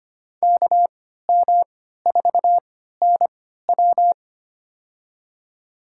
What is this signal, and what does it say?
Morse code, heard as a keyed tone, about 25 wpm, spaced as for 14 wpm, 700 Hz. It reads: KM4NW
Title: USA Call Signs - 5 Characters - 3x Character Spacing - 25wpm